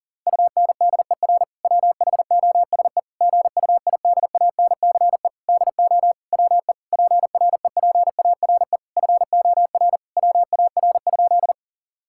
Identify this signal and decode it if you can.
Morse code, heard as a keyed tone, about 40 wpm, 700 Hz. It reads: UNDER WHOSE GUIDANCE DO WE PREPARE FOR WAR?
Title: Sentences from Top 1600 Words 40wpm